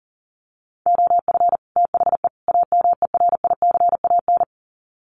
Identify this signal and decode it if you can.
Morse code, heard as a keyed tone, about 40 wpm, 700 Hz. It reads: OF THE AMERICAN
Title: Sets of 3 Words 40wpm